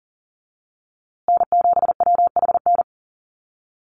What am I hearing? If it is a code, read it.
Morse code, heard as a keyed tone, about 40 wpm, 700 Hz. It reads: N7WHN
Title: USA Call Signs Repeated 2x 40wpm